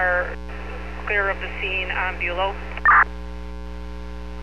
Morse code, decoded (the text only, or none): none